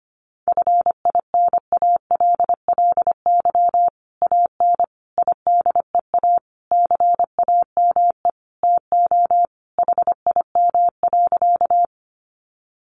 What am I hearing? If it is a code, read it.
Morse code, heard as a keyed tone, about 25 wpm, 700 Hz. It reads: FINALLY AN IDEA CAME TO HIM.
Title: Sentences from Top 1000 Words 25wpm